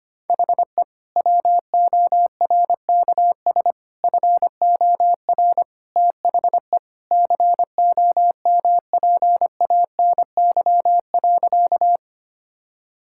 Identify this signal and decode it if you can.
Morse code, heard as a keyed tone, about 25 wpm, 700 Hz. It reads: HE WORKS FOR THE COMPANY.
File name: practice / 25wpm / Sentences with 2-Word Phrases - 25wpm